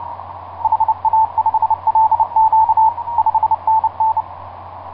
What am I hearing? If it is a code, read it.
Morse code, heard as a keyed tone, about 30 wpm, 880 Hz. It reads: HA5LQ 5NN
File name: 8q7qx dk7lx etc 12m cw 30mar2010 1141